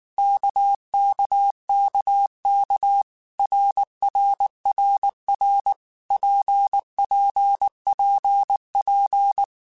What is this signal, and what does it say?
Morse code, heard as a keyed tone, about 19 wpm, 800 Hz. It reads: KKKK RRRR PPPP
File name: Basic Letter Groups By7Groups / RKPX / RKPX@18